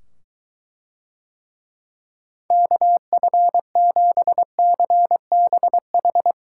Morse code, heard as a keyed tone, about 23 wpm, 700 Hz. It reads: KF7CBH